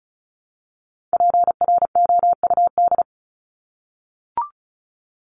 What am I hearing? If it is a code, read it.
Morse code, heard as a keyed tone, about 34 wpm, 700 Hz. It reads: PROUD
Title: Top 1501-1750 Words 35wpm